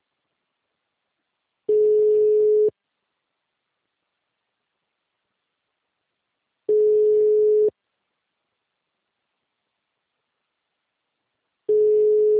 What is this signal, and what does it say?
Ringback tone (Europe, 425 Hz).
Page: Dracena (Call-Center)